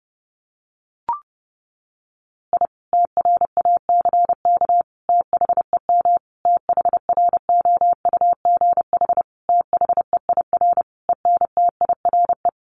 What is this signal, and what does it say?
Morse code, heard as a keyed tone, about 30 wpm, 700 Hz. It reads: I TRACK THEM THROUGH THEIR ENTIRE
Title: A Sound Of Thunder 30wpm